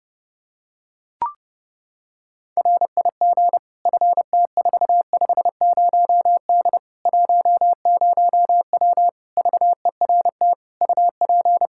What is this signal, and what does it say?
Morse code, heard as a keyed tone, about 32 wpm, 700 Hz. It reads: RIG FT450D 10W VERT UP
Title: Practice QSO - Exchange 2 - Info - 30wpm